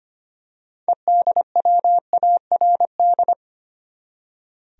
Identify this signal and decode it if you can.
Morse code, heard as a keyed tone, about 25 wpm, 700 Hz. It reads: EDWARD
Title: USA Common Names - 25wpm